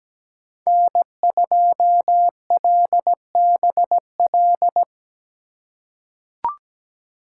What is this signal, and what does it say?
Morse code, heard as a keyed tone, about 17 wpm, 700 Hz. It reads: N2LBL
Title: USA Call Signs - 5 Characters - 4x Character Spacing - 17wpm